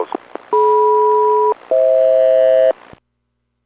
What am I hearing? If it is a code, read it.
Aircraft SELCAL, heard as two pairs of tones. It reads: DM-FH